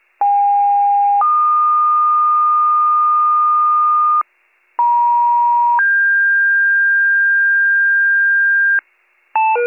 Two-tone page, heard as a one-second tone, then a three-second tone, about 810 then 1220 Hz, about 950 then 1600 Hz, with noise. It is heard from 0.2 to 8.8 s.